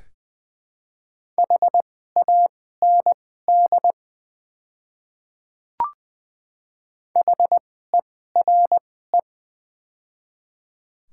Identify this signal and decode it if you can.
Morse code, heard as a keyed tone, about 20 wpm, spaced as for 14 wpm, 700 Hz. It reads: HAND HERE
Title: 2x Character Spacing - Top 300 Words - 20wpm